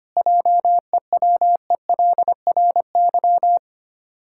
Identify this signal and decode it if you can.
Morse code, heard as a keyed tone, about 25 wpm, 700 Hz. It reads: JEWELRY
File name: practice / 25wpm / Difficult Words with Hints - 25wpm